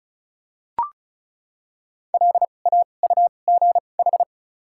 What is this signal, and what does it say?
Morse code, heard as a keyed tone, about 34 wpm, spaced as for 25 wpm, 700 Hz. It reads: LAUGH
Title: 2x Character Spacing - Top 500 Words - 35wpm